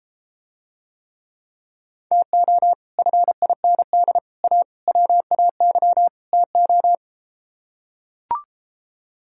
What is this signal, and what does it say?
Morse code, heard as a keyed tone, about 33 wpm, 700 Hz. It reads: TO FIND A WAY TO